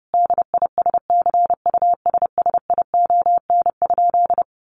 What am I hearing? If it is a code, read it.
Morse code, heard as a keyed tone, about 28 wpm, 700 Hz. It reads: DISCUSSION?
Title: Sentences from Top 1100 Words 30wpm